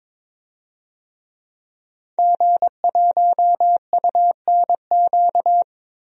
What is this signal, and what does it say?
Morse code, heard as a keyed tone, about 22 wpm, 700 Hz. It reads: G1UNQ
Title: DXCC-100 Call Signs - 5 Characters - Repeated 3x - 22wpm